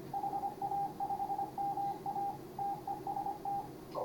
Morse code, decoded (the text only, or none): HA5MA TEST